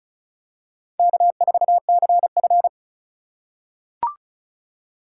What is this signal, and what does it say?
Morse code, heard as a keyed tone, about 34 wpm, 700 Hz. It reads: K4CF
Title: USA Call Signs - 4 Characters - 4x Character Spacing - 35wpm